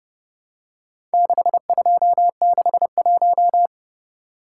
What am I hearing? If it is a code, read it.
Morse code, heard as a keyed tone, about 31 wpm, 700 Hz. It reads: 6261